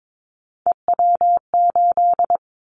Morse code, heard as a keyed tone, about 22 wpm, 700 Hz. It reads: EW8
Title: Callsign Prefixes for Top 50 DXCC - 22wpm